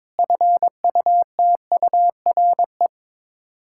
Morse code, heard as a keyed tone, about 22 wpm, 700 Hz. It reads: FUTURE
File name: practice / 22wpm / Sets of 6 Words - 22wpm